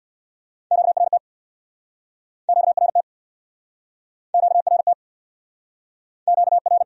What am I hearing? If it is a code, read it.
Morse code, heard as a keyed tone, about 62 wpm, 700 Hz. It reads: QRT QRT QRT QR
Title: Speed-Racing Q-Codes - 50wpm